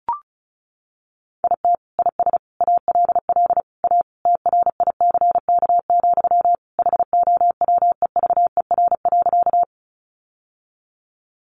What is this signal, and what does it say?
Morse code, heard as a keyed tone, about 34 wpm, 700 Hz. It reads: IT IS ALL A TRICK, HOWEVER.
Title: Sentences from Top 1500 Words 35wpm